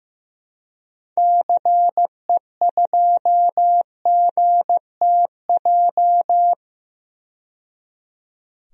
Morse code, heard as a keyed tone, about 15 wpm, 700 Hz. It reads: CE2GTJ
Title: DXCC-100 Call Signs - 6 Characters - 15wpm